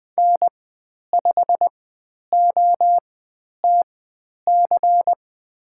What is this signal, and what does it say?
Morse code, heard as a keyed tone, about 20 wpm, spaced as for 10 wpm, 700 Hz. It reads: N5OTC
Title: 109 Lesson 18g - Letter m - Call Signs 20wpm